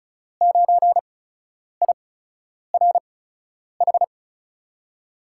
Morse code, heard as a keyed tone, about 36 wpm, spaced as for 10 wpm, 700 Hz. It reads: 9IRH
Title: USA Call Signs - 6 Characters - 8x Character Spacing - 35wpm